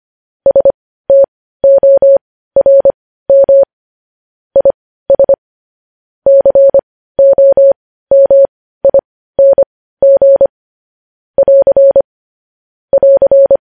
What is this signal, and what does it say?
Morse code, heard as a keyed tone, about 25 wpm, spaced as for 15 wpm, 550 Hz. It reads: STORM IS COMING <AR> <AR>